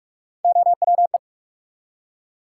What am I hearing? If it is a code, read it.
Morse code, heard as a keyed tone, about 41 wpm, 700 Hz. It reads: OWE
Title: Three Letter Words - 45wpm